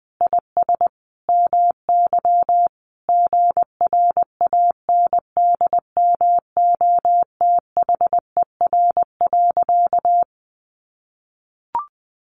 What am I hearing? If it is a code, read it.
Morse code, heard as a keyed tone, about 20 wpm, 700 Hz. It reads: IS MY GRANDMOTHER.
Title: Sentences from Top 1200 Words 20wpm